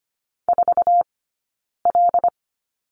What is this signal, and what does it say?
Morse code, heard as a keyed tone, about 25 wpm, 700 Hz. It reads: 4 L
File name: Lesson 17d - Letter u - Two Characters 25wpm